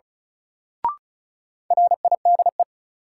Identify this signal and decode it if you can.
Morse code, heard as a keyed tone, about 36 wpm, 700 Hz. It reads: RIDE
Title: Sentences from Top 1000 Words - Review Part 6 - New Word Drill - 35wpm